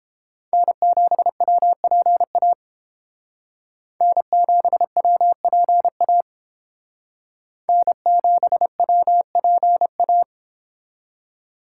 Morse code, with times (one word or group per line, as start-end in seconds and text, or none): 0.5-2.5 N7WPA
4.0-6.2 N7WPA
7.7-10.2 N7WPA